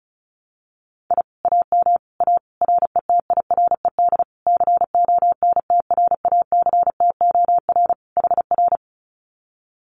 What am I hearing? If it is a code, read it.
Morse code, heard as a keyed tone, about 34 wpm, 700 Hz. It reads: I AM A RETIRED CONTRACTOR HR
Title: Common QSO Phrases Encore 35wpm